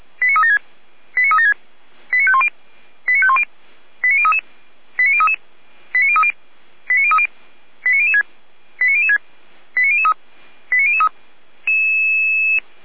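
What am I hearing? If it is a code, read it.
Five-tone selective call (ZVEI), heard as quick runs of five tones, tones of about 70 ms. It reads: 89376 89376 89510 89510 89033 89033 89030 89030 89006 89006 89003 89003
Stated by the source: ZVEI five-tone call received by SDR, Feuerwehr Orlamünde.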